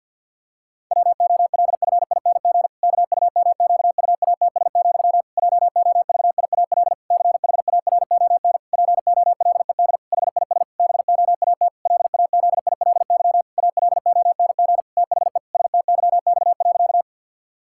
Morse code, heard as a keyed tone, about 50 wpm, 700 Hz. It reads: WOLFING KUMQUATS, JOVIAL CHARON POLED HIS BOAT LAZILY ALONG THE STYX.
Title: Pangram Challenge v2 - 50wpm